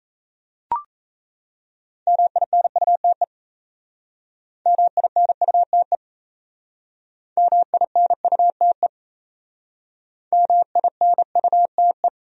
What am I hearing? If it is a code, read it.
Morse code, heard as a keyed tone, about 37 wpm, 700 Hz. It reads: MINUTE MINUTE MINUTE MINUTE